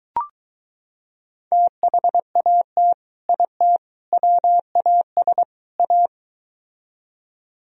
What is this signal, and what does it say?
Morse code, heard as a keyed tone, about 22 wpm, 700 Hz. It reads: THAT IT WAS A